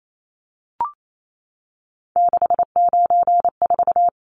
Morse code, heard as a keyed tone, about 28 wpm, 700 Hz. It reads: 694
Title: Numbers - 3-Digit 28wpm